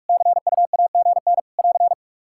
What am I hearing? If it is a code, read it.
Morse code, heard as a keyed tone, about 47 wpm, 700 Hz. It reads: CUAGN <AR>
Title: Practice QSO - Exchange 4 - Ending - 45wpm